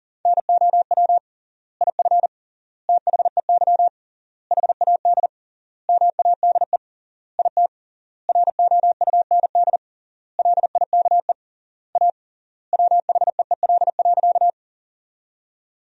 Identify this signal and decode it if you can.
Morse code, heard as a keyed tone, about 40 wpm, 700 Hz. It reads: NOW IF THEY HAD MADE IT ROUND LIKE A WHEEL.
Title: 3x Word-Spacing - Sentences from Top 500 Words - 40wpm